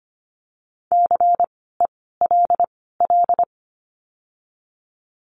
Morse code, heard as a keyed tone, about 25 wpm, spaced as for 16 wpm, 700 Hz. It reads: CELL
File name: course / 25:10wpm / 094 Lesson 16f - Letter c - Words 25wpm